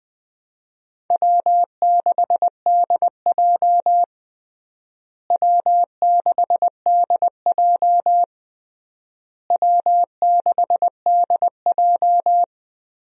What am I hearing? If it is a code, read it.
Morse code, heard as a keyed tone, about 21 wpm, 700 Hz. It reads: W6DJ W6DJ W6DJ
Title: USA Call Signs - 4 Characters - Repeated 3x - 20wpm